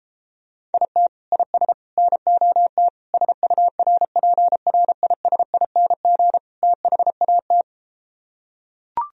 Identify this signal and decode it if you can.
Morse code, heard as a keyed tone, about 33 wpm, 700 Hz. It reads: IT IS NOT SURPRISING THAT